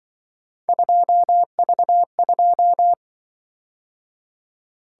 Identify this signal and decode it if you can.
Morse code, heard as a keyed tone, about 25 wpm, 700 Hz. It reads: 2V2
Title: Speed-Racing - Three Letter-Number Combinations - 20wpm